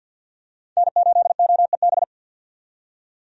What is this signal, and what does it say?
Morse code, heard as a keyed tone, about 48 wpm, 700 Hz. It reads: N8OEB
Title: USA Call Signs - 5 Characters - 50wpm